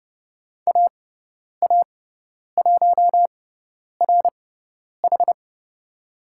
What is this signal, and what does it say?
Morse code, heard as a keyed tone, about 30 wpm, spaced as for 10 wpm, 700 Hz. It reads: AA1RH
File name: Lesson 17g - Letter u - Call Signs 30wpm